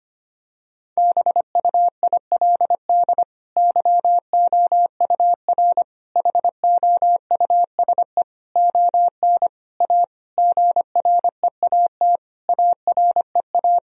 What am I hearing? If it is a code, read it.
Morse code, heard as a keyed tone, about 25 wpm, 700 Hz. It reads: BUILD YOUR HOUSE ON A GREAT AREA
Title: Sentences from Top 1000 Words - Review Part 2 - 25wpm